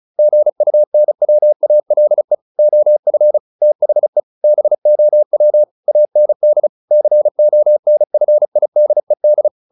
Morse code, heard as a keyed tone, about 34 wpm, 600 Hz. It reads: GUNWALE OF THE BOW AND CONFIDED
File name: the old man and the sea 2 35wpm